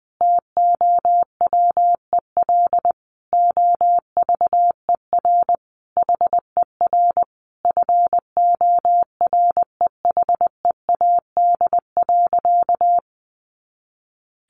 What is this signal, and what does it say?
Morse code, heard as a keyed tone, about 20 wpm, 700 Hz. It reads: TOWEL OVER HER FOREHEAD.
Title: Easy Phrases 20wpm